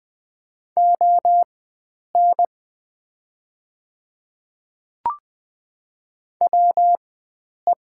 Morse code, heard as a keyed tone, about 20 wpm, spaced as for 9 wpm, 700 Hz. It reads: ON WE
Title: 4x Character Spacing - Top 100 Words - 20wpm